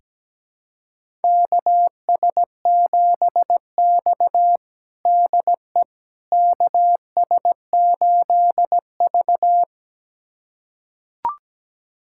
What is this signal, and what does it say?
Morse code, heard as a keyed tone, about 17 wpm, 700 Hz. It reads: KS7X DE KS8V
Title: USA Callsign DE Callsign - 4 Characters - 17wpm